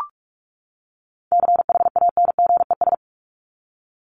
Morse code, heard as a keyed tone, about 44 wpm, 700 Hz. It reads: CHANGES
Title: ES:S Suffix - Top 100 Words with Base Words 45wpm